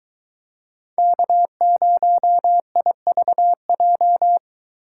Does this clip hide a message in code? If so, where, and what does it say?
Yes: Morse code from 1.0 to 4.4 s, heard as a keyed tone, about 22 wpm, 700 Hz. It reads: K0IVJ